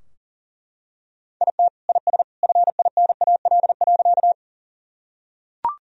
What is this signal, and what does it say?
Morse code, heard as a keyed tone, about 40 wpm, 700 Hz. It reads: IT IS FINAL.
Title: Sentences from Top 1000 Words - Review Part 2 - 40wpm